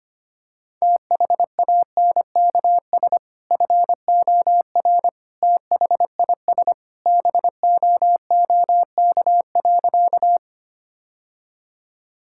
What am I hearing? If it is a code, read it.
Morse code, heard as a keyed tone, about 25 wpm, 700 Hz. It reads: THANKS FOR THIS BOOK.
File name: practice / 25wpm / Sentences with 2-Word Phrases - 25wpm